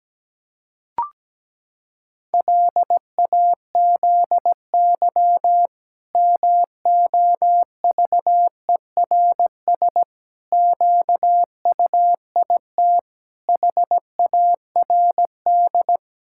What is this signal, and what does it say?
Morse code, heard as a keyed tone, about 17 wpm, 700 Hz. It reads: LAZY MOVERS QUIT HARD
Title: Pangram Challenge v2 - 17wpm